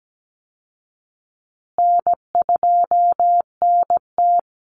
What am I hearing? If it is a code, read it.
Morse code, heard as a keyed tone, about 17 wpm, 700 Hz. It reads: N2NT
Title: Easy Call Signs Repeated 3x - 17wpm